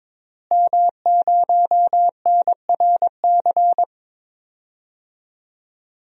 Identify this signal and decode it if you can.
Morse code, heard as a keyed tone, about 22 wpm, 700 Hz. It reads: M0NRC
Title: DXCC-100 Call Signs - 5 Characters - Repeated 3x - 22wpm